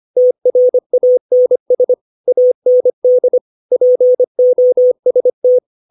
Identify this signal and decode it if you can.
Morse code, heard as a keyed tone, about 25 wpm, 500 Hz. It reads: TRANS AND POST
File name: 632 words / 632 Words 25WPM